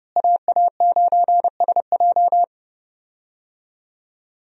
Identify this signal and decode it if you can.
Morse code, heard as a keyed tone, about 30 wpm, 700 Hz. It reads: AA9SJ
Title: USA Call Signs - 5 Characters - Repeated 2x - 30wpm